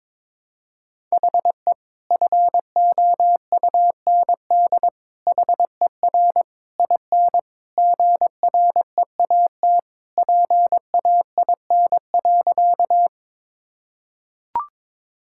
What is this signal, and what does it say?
Morse code, heard as a keyed tone, about 21 wpm, 700 Hz. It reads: HE FOUND HER IN GREAT PAIN.
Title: Sentences from Top 1600 Words 22wpm